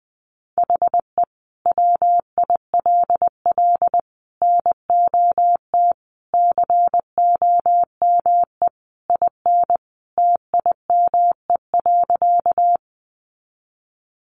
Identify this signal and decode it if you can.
Morse code, heard as a keyed tone, about 19 wpm, 700 Hz. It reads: HE WILL NOT COME IN TIME.
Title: Sentences from Top 600 Words 20wpm